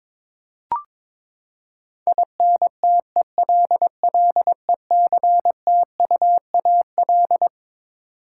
Morse code, heard as a keyed tone, about 22 wpm, 700 Hz. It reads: INTELLECTUAL
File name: Twelve Letter Words - 22wpm